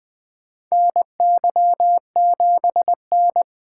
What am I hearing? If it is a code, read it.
Morse code, heard as a keyed tone, about 21 wpm, 700 Hz. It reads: NY7N
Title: K1USN SST QSO Practice - 20wpm